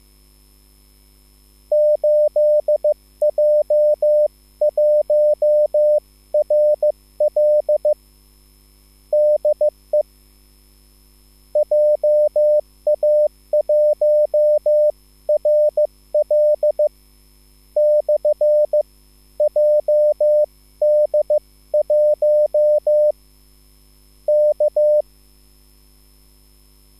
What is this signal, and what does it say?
Morse code, heard as a keyed tone, about 15 wpm, 600 Hz. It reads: 8J1RL DE JA1RL / JD1 K